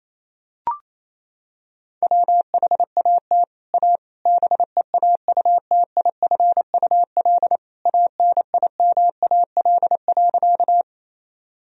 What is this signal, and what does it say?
Morse code, heard as a keyed tone, about 28 wpm, 700 Hz. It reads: WHAT A BEAUTIFUL ANIMAL.
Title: Sentences from Top 1000 Words - Review Part 1 - 28wpm